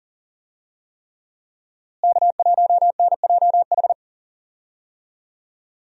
Morse code, heard as a keyed tone, about 39 wpm, 700 Hz. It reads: K1NJH
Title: USA Call Signs - 5 Characters - Repeated 3x - 40wpm